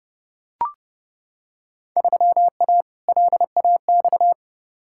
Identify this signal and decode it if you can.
Morse code, heard as a keyed tone, about 30 wpm, 700 Hz. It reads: 3A LAX
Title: Field Day Exchanges - 30wpm